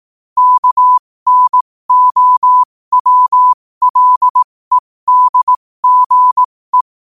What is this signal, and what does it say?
Morse code, heard as a keyed tone, about 18 wpm, spaced as for 15 wpm, 1000 Hz. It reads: KNOWLEDGE
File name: qotdfiles / 15wpm